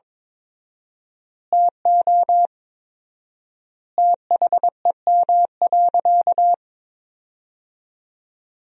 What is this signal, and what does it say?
Morse code, heard as a keyed tone, about 22 wpm, 700 Hz. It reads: TO THEM.
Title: 4x Word-Spacing - Sentences from Top 200 Words - 22wpm